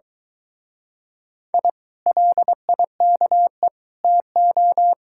Morse code, heard as a keyed tone, about 24 wpm, 700 Hz. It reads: I LIKE TO